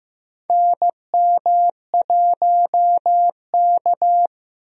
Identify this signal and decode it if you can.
Morse code, heard as a keyed tone, about 15 wpm, 700 Hz. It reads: NM1K
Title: USA Call Signs - 4 Characters - Repeated 3x - 15wpm